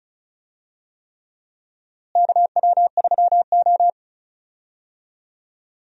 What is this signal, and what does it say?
Morse code, heard as a keyed tone, about 34 wpm, 700 Hz. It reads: KW3O
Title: USA Call Signs - 4 Characters - Repeated 3x - 35wpm